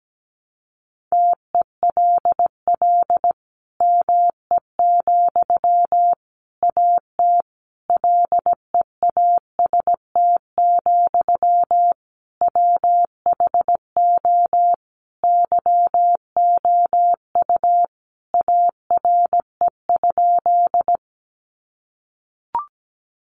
Morse code, heard as a keyed tone, about 17 wpm, 700 Hz. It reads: TELL ME, AT LEAST, WHO YOU ARE?
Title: Sentences from Top 700 Words 17wpm